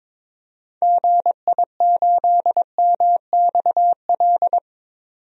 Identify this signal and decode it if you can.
Morse code, heard as a keyed tone, about 21 wpm, 700 Hz. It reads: GI8MXL
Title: DXCC-100 Call Signs - 6 Characters - 22wpm